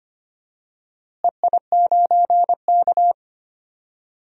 Morse code, heard as a keyed tone, about 25 wpm, 700 Hz. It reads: EI9K
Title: DXCC-100 Call Signs - 4 Characters - Repeated 3x - 25wpm